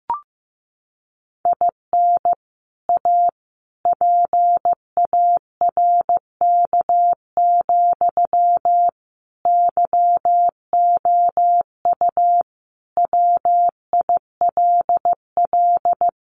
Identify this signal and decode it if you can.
Morse code, heard as a keyed tone, about 15 wpm, 700 Hz. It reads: IN A PARK, YOU WILL
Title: Sentences from Top 1300 Words 15wpm